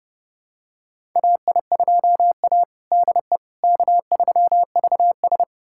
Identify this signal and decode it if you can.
Morse code, heard as a keyed tone, about 30 wpm, 700 Hz. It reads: AI2A DE K3VS
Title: USA Callsign DE Callsign - 4 Characters - 30wpm